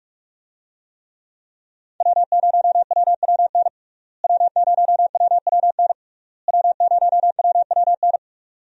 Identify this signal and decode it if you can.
Morse code, heard as a keyed tone, about 46 wpm, 700 Hz. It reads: W0WWN W0WWN W0WWN
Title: USA Call Signs - 5 Characters - Repeated 3x - 45wpm